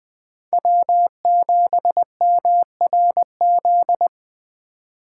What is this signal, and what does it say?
Morse code, heard as a keyed tone, about 20 wpm, 700 Hz. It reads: W7MRZ